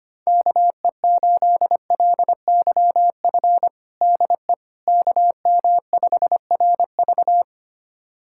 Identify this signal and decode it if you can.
Morse code, heard as a keyed tone, about 25 wpm, 700 Hz. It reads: KE8LYF DE KM5RV
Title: USA Callsign DE Callsign 25wpm